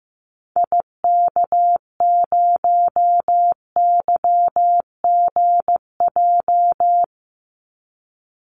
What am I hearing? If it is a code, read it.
Morse code, heard as a keyed tone, about 15 wpm, 700 Hz. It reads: IK0YGJ
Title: Call Signs Repeated 2x - 15wpm